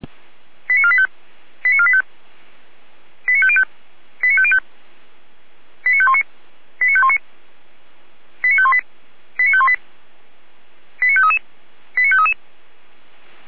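Five-tone selective call (ZVEI), heard as quick runs of five tones, tones of about 70 ms. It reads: 89485 89485 89505 89505 89519 89519 89518 89518 89533 89533
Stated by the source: ZVEI five-tone selective call received on SDR.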